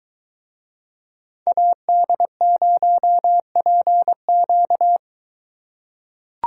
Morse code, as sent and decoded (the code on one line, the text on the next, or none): .- -.. ----- .--. --.-
AD0PQ